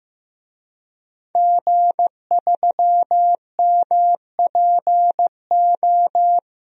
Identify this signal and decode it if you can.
Morse code, heard as a keyed tone, about 15 wpm, 700 Hz. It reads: G3MPO
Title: DXCC-100 Call Signs - 5 Characters - 15wpm